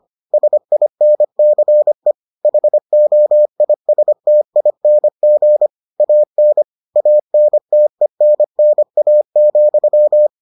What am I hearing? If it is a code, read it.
Morse code, heard as a keyed tone, about 25 wpm, 600 Hz. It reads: SINCE HOISTING AN ANTENNA,